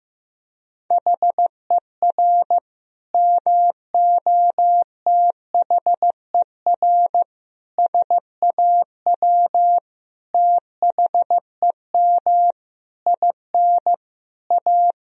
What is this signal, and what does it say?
Morse code, heard as a keyed tone, about 15 wpm, 700 Hz. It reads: HER MOTHER SAW THEM IN A
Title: Sentences from Top 1000 Words - Review Part 3 - 15wpm